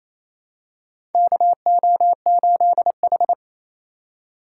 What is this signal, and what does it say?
Morse code, heard as a keyed tone, about 28 wpm, 700 Hz. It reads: KO8H
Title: USA Call Signs - 4 Characters - 2x Character Spacing - 28wpm